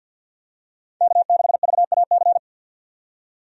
Morse code, heard as a keyed tone, about 46 wpm, 700 Hz. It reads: K6VAC
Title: USA Call Signs - 5 Characters - 50wpm